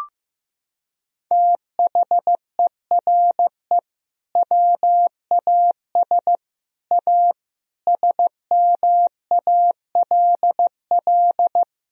Morse code, heard as a keyed tone, about 15 wpm, 700 Hz. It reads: THERE WAS A SMALL
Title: Sentences from Top 1000 Words - Review Part 1 - 15wpm